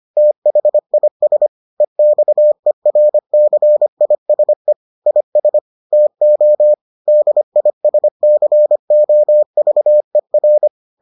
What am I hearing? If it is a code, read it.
Morse code, heard as a keyed tone, about 25 wpm, 600 Hz. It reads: THIS EXERCISE IS TO DISCOVER